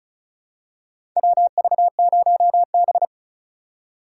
Morse code, heard as a keyed tone, about 34 wpm, 700 Hz. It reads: WV0B
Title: USA Call Signs - 4 Characters - 2x Character Spacing - 35wpm